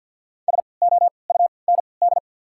Morse code, heard as a keyed tone, about 53 wpm, spaced as for 29 wpm, 690 Hz. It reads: SOUND
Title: 3x Character Spacing - Top 100 Words - 50wpm